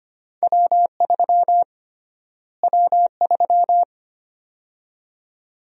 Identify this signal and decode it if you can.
Morse code, heard as a keyed tone, about 25 wpm, 700 Hz. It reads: W3 W3
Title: USA Call Sign Prefixes - Repeated 2x - 25wpm